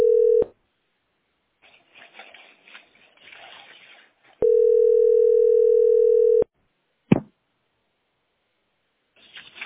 Ringback tone (North America), 440 + 480 Hz.